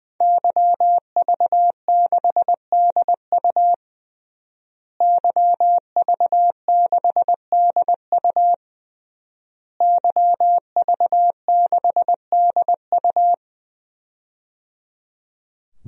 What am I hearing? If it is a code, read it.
Morse code, heard as a keyed tone, about 20 wpm, 700 Hz. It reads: YV6DU YV6DU YV6DU
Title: DXCC-100 Call Signs - 5 Characters - Repeated 3x - 20wpm